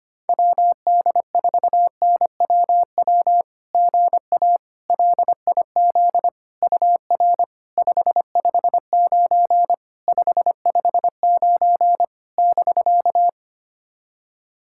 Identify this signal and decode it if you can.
Morse code, heard as a keyed tone, about 25 wpm, 690 Hz. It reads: WD4NWW GA LIZ UR 559 559 <BK>
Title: SOTA QSO Practice 25wpm